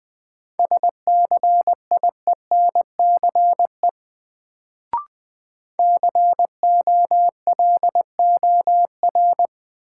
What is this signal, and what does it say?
Morse code, heard as a keyed tone, about 20 wpm, 700 Hz. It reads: SCIENCE COLOR
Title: Sentences from Top 1000 Words - Review Part 2 - New Word Drill - 20wpm